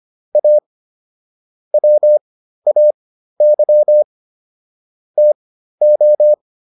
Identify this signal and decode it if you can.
Morse code, heard as a keyed tone, about 25 wpm, spaced as for 13 wpm, 600 Hz. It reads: A WAY TO